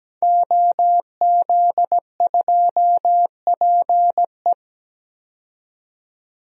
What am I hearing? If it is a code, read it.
Morse code, heard as a keyed tone, about 17 wpm, 700 Hz. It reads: OZ2PE
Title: DXCC-100 Call Signs - 5 Characters - 17wpm